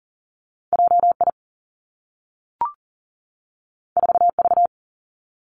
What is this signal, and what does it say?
Morse code, heard as a keyed tone, about 40 wpm, 700 Hz. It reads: J I 4 V
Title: Two Challenging Letter-Number Combinations 40wpm